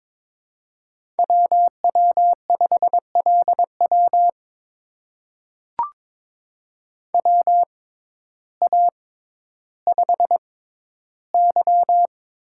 Morse code, heard as a keyed tone, about 22 wpm, spaced as for 12 wpm, 700 Hz. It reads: WW5LW WA5Y
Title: USA Call Signs - 6x Character Spacing - 22wpm